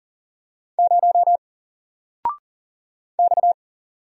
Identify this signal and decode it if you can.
Morse code, heard as a keyed tone, about 40 wpm, 700 Hz. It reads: 0 X
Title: Single Letter-Number 40wpm